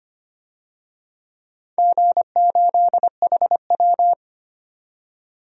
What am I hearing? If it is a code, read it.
Morse code, heard as a keyed tone, about 25 wpm, 700 Hz. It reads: G8HW
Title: DXCC-100 Call Signs - 4 Characters - 25wpm